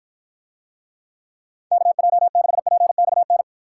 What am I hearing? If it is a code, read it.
Morse code, heard as a keyed tone, about 53 wpm, 700 Hz. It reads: KJ6PXN